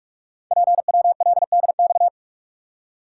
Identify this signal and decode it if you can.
Morse code, heard as a keyed tone, about 44 wpm, 700 Hz. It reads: PWLDX
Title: Code Groups - 5 Letter:Numbers - 45wpm